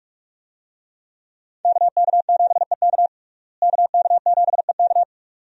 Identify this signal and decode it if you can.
Morse code, heard as a keyed tone, about 45 wpm, 700 Hz. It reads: KK7EK KK7EK
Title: USA Call Signs - 5 Characters - Repeated 2x - 45wpm